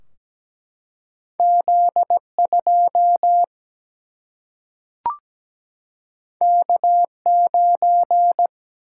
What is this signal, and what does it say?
Morse code, heard as a keyed tone, about 17 wpm, 700 Hz. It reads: Z2 K9
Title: Single-Letter Single-Number 17wpm